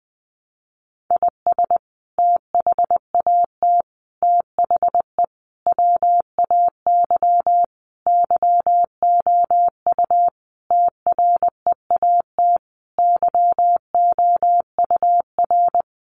Morse code, heard as a keyed tone, about 20 wpm, 700 Hz. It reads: IS THAT THE WAY YOU TREAT YOUR
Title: Sentences from Top 1300 Words 20wpm